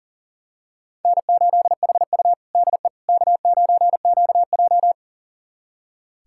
Morse code, heard as a keyed tone, about 40 wpm, 700 Hz. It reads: N8HU DE K9QJ
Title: USA Callsign DE Callsign - 4 Characters - 40wpm